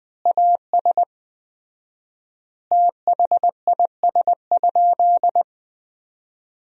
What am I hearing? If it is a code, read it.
Morse code, heard as a keyed tone, about 20 wpm, 700 Hz. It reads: AS THIS?
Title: 4x Word-Spacing - Sentences from Top 400 Words - 20wpm